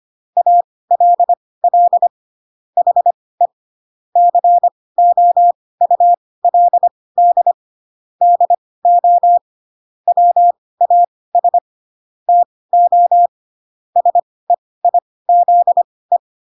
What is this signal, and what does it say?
Morse code, heard as a keyed tone, about 25 wpm, spaced as for 18 wpm, 700 Hz. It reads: ALL HE COULD DO WAS TO SEIZE
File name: Pinocchio ch3 12 25 ik1hge